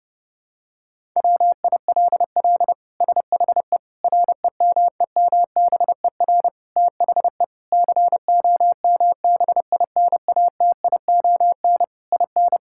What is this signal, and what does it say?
Morse code, heard as a keyed tone, about 29 wpm, 700 Hz. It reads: WILL SHE REMEMBER THE COMBINATION IN
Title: Sentences from Top 1000 Words - Review Part 6 - 30wpm